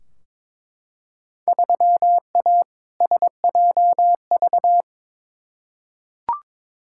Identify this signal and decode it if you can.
Morse code, heard as a keyed tone, about 22 wpm, 700 Hz. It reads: 3A SJV
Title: Field Day Exchanges Encore - 22wpm